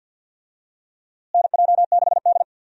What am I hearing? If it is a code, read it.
Morse code, heard as a keyed tone, about 50 wpm, 700 Hz. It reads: NJ6D
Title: USA Call Signs - 4 Characters - Repeated 2x - 50wpm